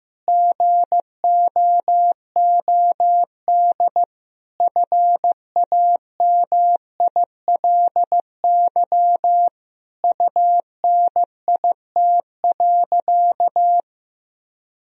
Morse code, heard as a keyed tone, about 16 wpm, 700 Hz. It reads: GOOD FAMILY UNIT.
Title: Sentences from Top 1000 Words - Review Part 2 - 15wpm